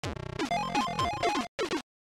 There is no code.